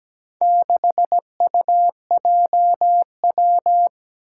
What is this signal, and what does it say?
Morse code, heard as a keyed tone, about 17 wpm, 700 Hz. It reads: 6UJW
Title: Four Challenging Letter-Number Combinations 17wpm